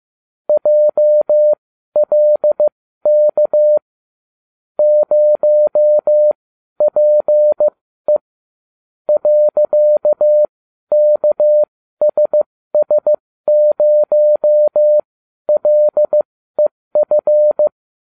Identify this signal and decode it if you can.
Morse code, heard as a keyed tone, about 15 wpm, spaced as for 12 wpm, 600 Hz. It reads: JLK 0PE .KSS0LEF